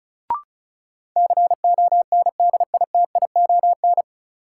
Morse code, heard as a keyed tone, about 36 wpm, 700 Hz. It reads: CONDITION
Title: Sentences from Top 1000 Words - New Word Drill - 35wpm